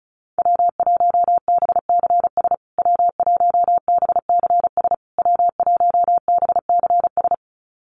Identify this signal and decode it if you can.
Morse code, heard as a keyed tone, about 34 wpm, 700 Hz. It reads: W1BCS W1BCS W1BCS
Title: USA Call Signs Repeated 3x 35wpm